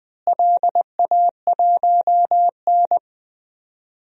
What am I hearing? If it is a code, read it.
Morse code, heard as a keyed tone, about 20 wpm, 700 Hz. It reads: LA1N
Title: DXCC-100 Call Signs - 4 Characters - Repeated 2x - 20wpm